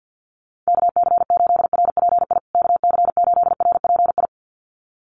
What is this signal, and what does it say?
Morse code, heard as a keyed tone, about 50 wpm, 700 Hz. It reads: KC8RPI KC8RPI
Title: USA Call Signs Repeated 2x 50wpm